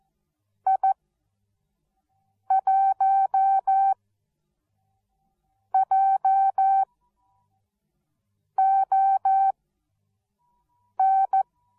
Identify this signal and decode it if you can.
Morse code, heard as a keyed tone, about 14 wpm, spaced as for 5 wpm, 770 Hz. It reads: I1JON